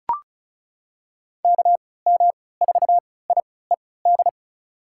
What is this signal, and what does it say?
Morse code, heard as a keyed tone, about 36 wpm, spaced as for 20 wpm, 700 Hz. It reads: KM4IED
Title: USA Call Signs - 6 Characters - 3x Character Spacing - 35wpm